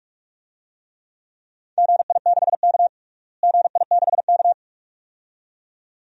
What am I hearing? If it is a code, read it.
Morse code, heard as a keyed tone, about 41 wpm, 700 Hz. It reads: GI6K GI6K
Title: DXCC-100 Call Signs - 4 Characters - Repeated 2x - 45wpm